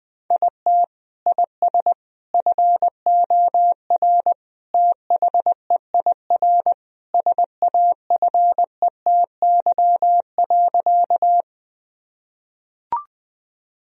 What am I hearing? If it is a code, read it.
Morse code, heard as a keyed tone, about 20 wpm, 700 Hz. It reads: IT IS FOR THEIR SAFETY.
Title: Sentences from Top 1000 Words - Review Part 4 - 20wpm